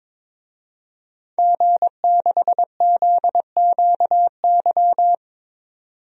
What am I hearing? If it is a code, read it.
Morse code, heard as a keyed tone, about 22 wpm, 700 Hz. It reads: G6ZQY
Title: DXCC-100 Call Signs - 5 Characters - 22wpm